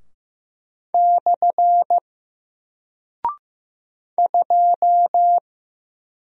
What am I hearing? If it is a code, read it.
Morse code, heard as a keyed tone, about 15 wpm, 700 Hz. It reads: / 2
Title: Single Character (Letter:Number:Punctuation:Prosign) 15wpm